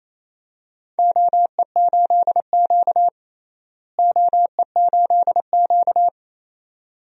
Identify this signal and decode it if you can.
Morse code, heard as a keyed tone, about 28 wpm, 700 Hz. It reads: OE8Q OE8Q
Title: DXCC-100 Call Signs - 4 Characters - Repeated 2x - 28wpm